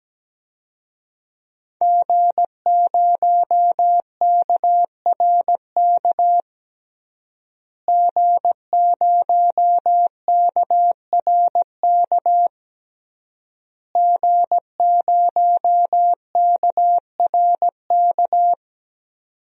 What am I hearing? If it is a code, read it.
Morse code, heard as a keyed tone, about 17 wpm, 700 Hz. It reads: G0KRK G0KRK G0KRK
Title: DXCC-100 Call Signs - 5 Characters - Repeated 3x - 17wpm